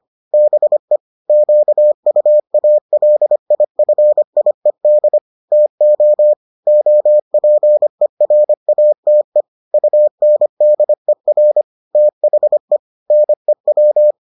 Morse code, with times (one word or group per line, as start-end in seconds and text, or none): 0.3-1.0 BE
1.3-5.2 QUALIFIED
5.5-6.3 TO
6.7-9.4 OPERATE
9.7-11.6 UNDER
11.9-12.8 THE
13.1-14.1 NEW